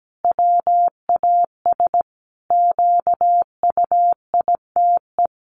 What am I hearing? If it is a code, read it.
Morse code, heard as a keyed tone, about 17 wpm, 700 Hz. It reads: WAS QUITE
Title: Sentences from Top 1100 Words 17wpm